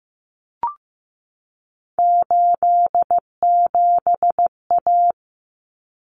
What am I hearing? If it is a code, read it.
Morse code, heard as a keyed tone, about 15 wpm, 700 Hz. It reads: 87A
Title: Common Cut Numbers - 3-Digit 15wpm